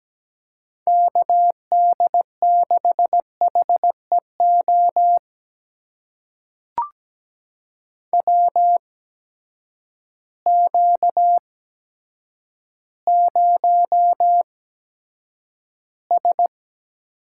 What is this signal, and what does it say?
Morse code, heard as a keyed tone, about 17 wpm, spaced as for 8 wpm, 700 Hz. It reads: KD6HEO WQ0S
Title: USA Call Signs - 8x Character Spacing - 17wpm